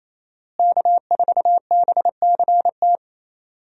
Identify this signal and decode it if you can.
Morse code, heard as a keyed tone, about 28 wpm, 700 Hz. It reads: K4BCT
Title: USA Call Signs - 5 Characters - Repeated 3x - 28wpm